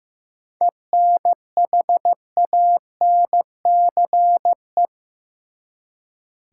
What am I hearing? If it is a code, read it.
Morse code, heard as a keyed tone, about 15 wpm, 700 Hz. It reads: ENHANCE
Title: Seven Letter Words - 15wpm